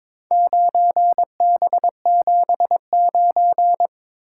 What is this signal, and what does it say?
Morse code, heard as a keyed tone, about 22 wpm, 700 Hz. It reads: 9B79
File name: practice / 22wpm / Four Letter-Number Combinations - 22wpm